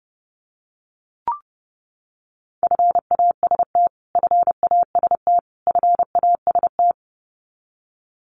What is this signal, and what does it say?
Morse code, heard as a keyed tone, about 30 wpm, 700 Hz. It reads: FAST FAST FAST
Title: Top 500 Words Repeated 30wpm